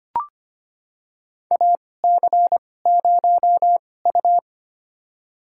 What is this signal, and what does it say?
Morse code, heard as a keyed tone, about 25 wpm, spaced as for 18 wpm, 700 Hz. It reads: AC0U
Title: USA Call Signs - 2x Character Spacing - 25wpm